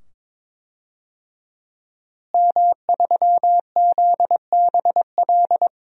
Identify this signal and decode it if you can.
Morse code, heard as a keyed tone, about 22 wpm, 700 Hz. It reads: M3ZBL
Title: DXCC-100 Call Signs - 5 Characters - Repeated 2x - 22wpm